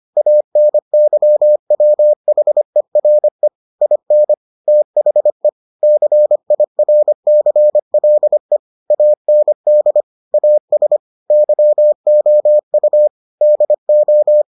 Morse code, heard as a keyed tone, about 25 wpm, 600 Hz. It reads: ANYWHERE IN THE CIRCLE AND AS YOU DO